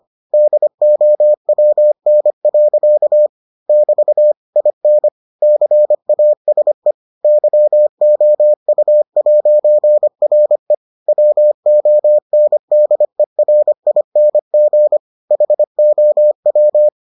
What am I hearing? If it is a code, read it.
Morse code, heard as a keyed tone, about 25 wpm, 600 Hz. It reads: DOWN. <BT> IN CASE YOU'RE WONDERING HOW